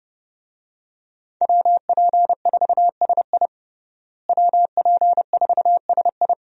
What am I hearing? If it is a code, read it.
Morse code, heard as a keyed tone, about 30 wpm, 700 Hz. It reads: WP4SI WP4SI
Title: USA Call Signs - 5 Characters - Repeated 3x - 30wpm